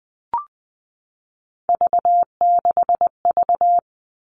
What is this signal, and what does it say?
Morse code, heard as a keyed tone, about 20 wpm, 700 Hz. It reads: V6V
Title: All Cut Numbers - 3-Digit 20wpm